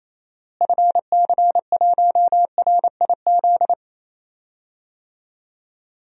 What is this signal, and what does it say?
Morse code, heard as a keyed tone, about 28 wpm, 700 Hz. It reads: FC1RIZ
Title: DXCC-100 Call Signs - 6 Characters - Repeated 2x - 28wpm